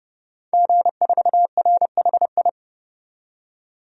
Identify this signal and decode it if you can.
Morse code, heard as a keyed tone, about 31 wpm, 700 Hz. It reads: G4RHI
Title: DXCC-100 Call Signs - 5 Characters - 30wpm